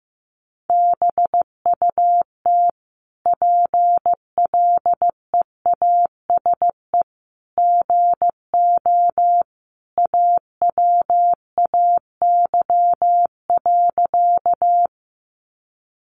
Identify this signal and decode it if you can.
Morse code, heard as a keyed tone, about 15 wpm, 700 Hz. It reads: BUT PLEASE GO AWAY.
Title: Sentences from Top 1500 Words 15wpm